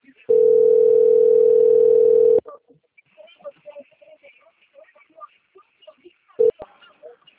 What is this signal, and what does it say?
Ringback tone (North America), 440 + 480 Hz.
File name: out- - - - - .12678